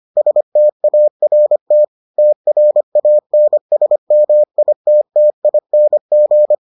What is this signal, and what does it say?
Morse code, heard as a keyed tone, about 25 wpm, 600 Hz. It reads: START TRANSMITTING